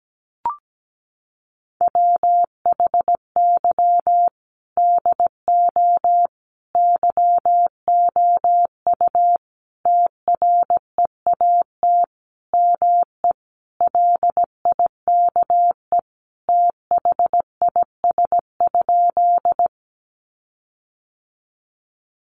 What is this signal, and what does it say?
Morse code, heard as a keyed tone, about 17 wpm, 700 Hz. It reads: WHY DO YOU TREAT ME LIKE THIS?
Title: Sentences from Top 1300 Words 17wpm